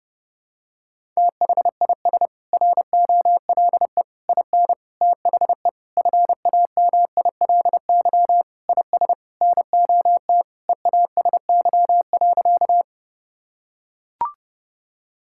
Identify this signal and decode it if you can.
Morse code, heard as a keyed tone, about 30 wpm, 700 Hz. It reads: THIS ROLE IN THE FAMILY IS NOT EASY.
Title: Sentences from Top 1000 Words - Review Part 6 - 30wpm